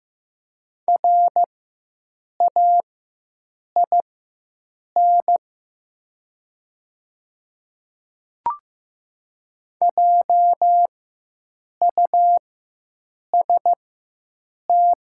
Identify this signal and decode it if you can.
Morse code, heard as a keyed tone, about 15 wpm, spaced as for 7 wpm, 700 Hz. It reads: RAIN JUST
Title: 4x Character Spacing - Top 300 Words - 15wpm